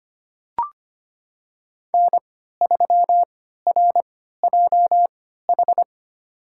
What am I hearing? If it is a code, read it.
Morse code, heard as a keyed tone, about 25 wpm, spaced as for 14 wpm, 700 Hz. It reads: N3RJH
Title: USA Call Signs - 3x Character Spacing - 25wpm